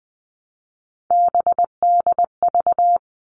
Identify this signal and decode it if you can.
Morse code, heard as a keyed tone, about 20 wpm, 700 Hz. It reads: BDV
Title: All Cut Numbers - 3-Digit 20wpm